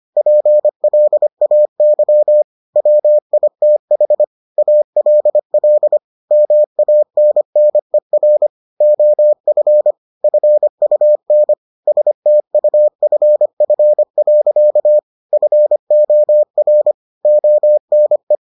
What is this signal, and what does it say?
Morse code, heard as a keyed tone, about 25 wpm, 600 Hz. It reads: PLAY WITH ALL MANNER OF FUN STUFF. FOR ONE